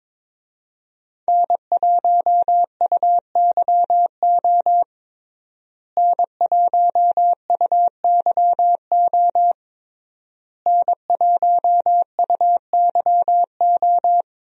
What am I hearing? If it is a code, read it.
Morse code, heard as a keyed tone, about 22 wpm, 700 Hz. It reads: N1UYO N1UYO N1UYO
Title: USA Call Signs - 5 Characters - Repeated 3x - 22wpm